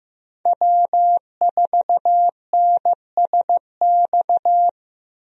Morse code, heard as a keyed tone, about 15 wpm, 700 Hz. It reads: W4NSX
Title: USA Call Signs - 5 Characters - Repeated 3x - 15wpm